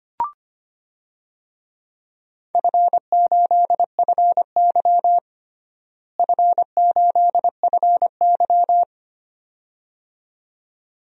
Morse code, heard as a keyed tone, about 25 wpm, 700 Hz. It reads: F8FY F8FY
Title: DXCC-100 Call Signs - 4 Characters - Repeated 2x - 25wpm